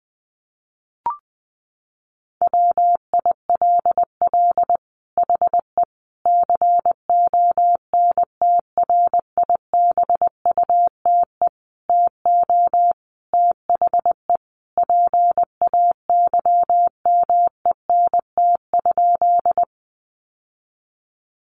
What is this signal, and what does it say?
Morse code, heard as a keyed tone, about 20 wpm, 700 Hz. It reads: WILL HE CONTRIBUTE TO THE PAYMENT?
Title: Sentences from Top 1400 Words 20wpm